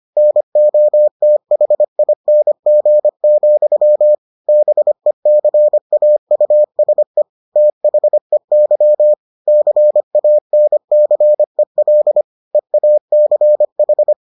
Morse code, heard as a keyed tone, about 25 wpm, 600 Hz. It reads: NOTHING, BECAUSE THEY CANCEL EACH